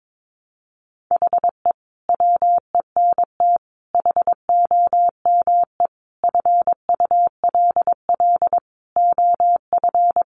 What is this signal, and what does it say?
Morse code, heard as a keyed tone, about 22 wpm, 700 Hz. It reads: HE WENT HOME FULL OF
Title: Sentences from Top 1300 Words 22wpm